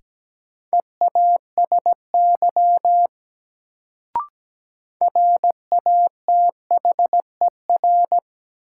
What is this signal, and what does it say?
Morse code, heard as a keyed tone, about 17 wpm, 700 Hz. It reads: EASY RATHER
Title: Sentences from Top 700 Words - New Word Drill - 17wpm